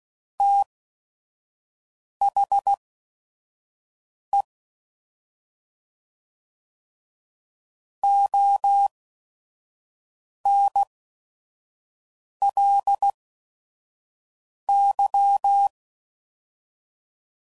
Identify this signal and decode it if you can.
Morse code, heard as a keyed tone, about 16 wpm, spaced as for 5 wpm, 800 Hz. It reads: THE ONLY
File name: CW Code Groups @5WPM / Technical Subjects / AND Gate